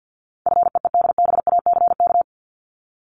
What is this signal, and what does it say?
Morse code, heard as a keyed tone, about 50 wpm, 700 Hz. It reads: FEEDBACK
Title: Top 1501-1750 Words 50wpm